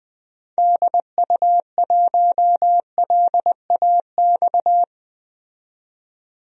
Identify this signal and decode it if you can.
Morse code, heard as a keyed tone, about 20 wpm, 700 Hz. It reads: DU1LAX
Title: DXCC-100 Call Signs - 6 Characters - 20wpm